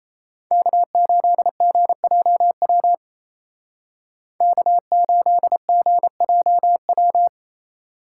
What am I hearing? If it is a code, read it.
Morse code, heard as a keyed tone, about 36 wpm, 700 Hz. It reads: K8GJW K8GJW